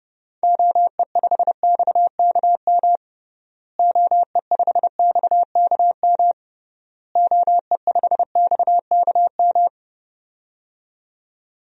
Morse code, heard as a keyed tone, about 32 wpm, 700 Hz. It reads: OE5XKM OE5XKM OE5XKM
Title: DXCC-100 Call Signs - 6 Characters - Repeated 3x - 30wpm